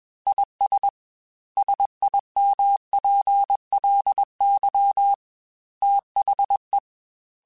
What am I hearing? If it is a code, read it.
Morse code, heard as a keyed tone, about 21 wpm, 800 Hz. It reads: IS SIMPLY THE